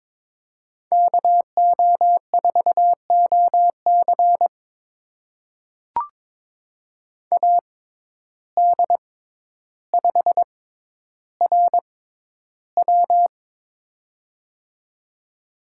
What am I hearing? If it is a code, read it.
Morse code, heard as a keyed tone, about 22 wpm, spaced as for 11 wpm, 700 Hz. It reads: KO4OC AD5RW
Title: USA Call Signs - 6x Character Spacing - 22wpm